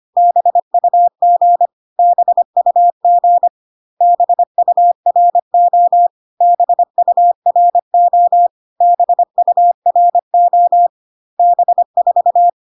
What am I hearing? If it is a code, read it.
Morse code, heard as a keyed tone, about 25 wpm, 700 Hz. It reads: BUG BUG BURO BURO BURO B4
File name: CW common abbreviations alphabetical repeated 3 times 25 25wpm ik1hge